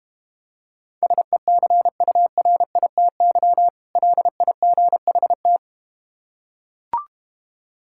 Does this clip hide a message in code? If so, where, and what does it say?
Yes: Morse code from 1.0 to 5.6 s, heard as a keyed tone, about 32 wpm, 700 Hz. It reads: SECURITY LIGHT